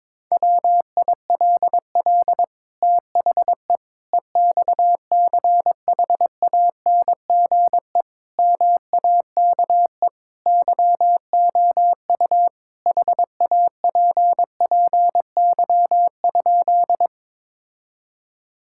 Morse code, heard as a keyed tone, about 22 wpm, 700 Hz. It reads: WILL THE EXCHANGE MAKE YOU HAPPY?
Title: Sentences from Top 1000 Words - Review Part 6 - 22wpm